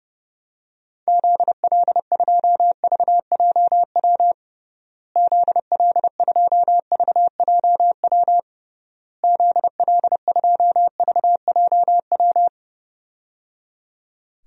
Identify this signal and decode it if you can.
Morse code, heard as a keyed tone, about 30 wpm, 700 Hz. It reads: ZL2VJW ZL2VJW ZL2VJW
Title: DXCC-100 Call Signs - 6 Characters - Repeated 3x - 30wpm